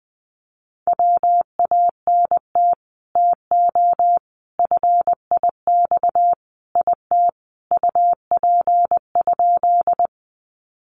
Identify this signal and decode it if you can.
Morse code, heard as a keyed tone, about 20 wpm, 700 Hz. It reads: WANT TO FIX IT UP?
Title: Sentences from Top 1400 Words 20wpm